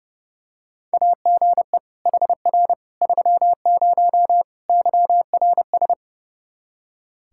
Morse code, heard as a keyed tone, about 31 wpm, 700 Hz. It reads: AGE HR 30 YRS
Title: Practice QSO - Exchange 3 - More Info - 30wpm